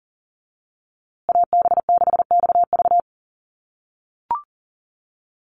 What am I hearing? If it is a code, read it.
Morse code, heard as a keyed tone, about 40 wpm, 700 Hz. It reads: AB6XV
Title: USA Call Signs Repeated 2x 40wpm